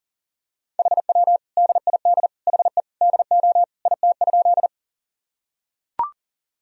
Morse code, heard as a keyed tone, about 40 wpm, 700 Hz. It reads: HW DID HE DO IT?
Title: Sentences from Top 100 Words and Abbreviations 40wpm